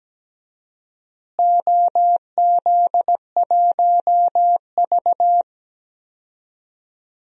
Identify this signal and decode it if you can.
Morse code, heard as a keyed tone, about 17 wpm, 700 Hz. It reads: OZ1V
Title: DXCC-100 Call Signs - 4 Characters - Repeated 2x - 17wpm